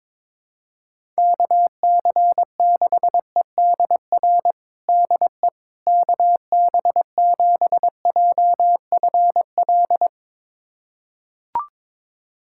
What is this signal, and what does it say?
Morse code, heard as a keyed tone, about 22 wpm, 700 Hz. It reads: KC6EDR DE KB7JFL
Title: USA Callsign DE Callsign 22wpm